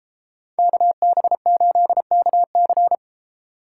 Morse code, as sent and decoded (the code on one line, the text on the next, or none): -.- -... ---.. -.- -.-.
KB8KC